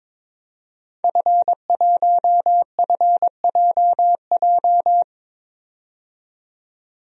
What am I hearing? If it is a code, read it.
Morse code, heard as a keyed tone, about 22 wpm, 700 Hz. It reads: F1FJJ
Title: DXCC-100 Call Signs - 5 Characters - 22wpm